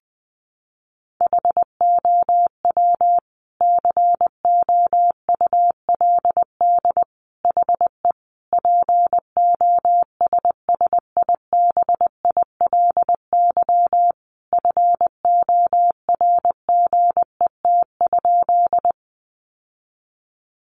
Morse code, heard as a keyed tone, about 20 wpm, 700 Hz. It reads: HOW COULD HE POSSIBILY FORGET?
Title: Sentences from Top 1300 Words 20wpm